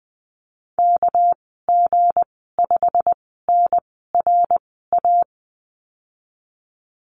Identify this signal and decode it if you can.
Morse code, heard as a keyed tone, about 20 wpm, 700 Hz. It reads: K G 5 N R A
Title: 200 Lesson 32g - Letter k - Call Signs 20wpm